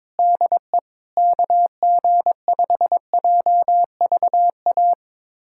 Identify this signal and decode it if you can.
Morse code, heard as a keyed tone, about 22 wpm, 700 Hz. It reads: DE KG5JVA
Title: USA Callsign DE Callsign - 6 Characters - 22wpm